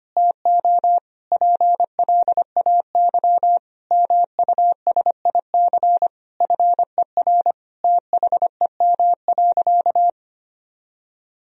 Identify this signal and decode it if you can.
Morse code, heard as a keyed tone, about 25 wpm, 700 Hz. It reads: TO PLAY MUSIC FER THEM.
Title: Sentences from Top 300 Words and Abbreviations 25wpm